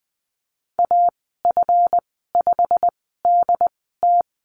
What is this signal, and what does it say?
Morse code, heard as a keyed tone, about 20 wpm, spaced as for 15 wpm, 700 Hz. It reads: AF5DT
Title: 140 Lesson 23g - Letter f - Call Signs 20wpm